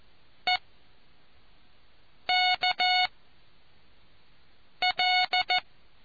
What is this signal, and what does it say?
Morse code, heard as a keyed tone, about 14 wpm, 730 Hz. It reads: E K L